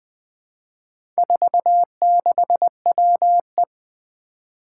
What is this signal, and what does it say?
Morse code, heard as a keyed tone, about 20 wpm, 700 Hz. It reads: 46WE
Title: Four Challenging Letter-Number Combinations 20wpm